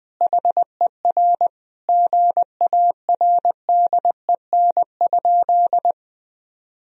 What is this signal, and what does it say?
Morse code, heard as a keyed tone, about 21 wpm, 700 Hz. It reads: HER GARDEN?
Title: Sentences from Top 1000 Words - Review Part 4 - 20wpm